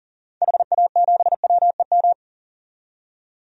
Morse code, heard as a keyed tone, about 40 wpm, 700 Hz. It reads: HA7WEM
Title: DXCC-100 Call Signs - 6 Characters - 40wpm